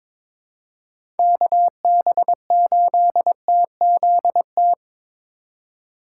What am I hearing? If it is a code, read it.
Morse code, heard as a keyed tone, about 22 wpm, 700 Hz. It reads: KB8TZT
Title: USA Call Signs - 3x Character Spacing - 22wpm